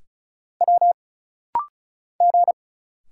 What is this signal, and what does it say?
Morse code, heard as a keyed tone, about 34 wpm, 700 Hz. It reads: W G
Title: Single Letters - Rapid-Fire 35wpm